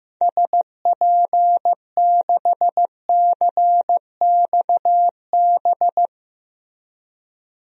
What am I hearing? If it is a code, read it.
Morse code, heard as a keyed tone, about 15 wpm, 700 Hz. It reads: SP6CXB
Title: DXCC-100 Call Signs - 6 Characters - 15wpm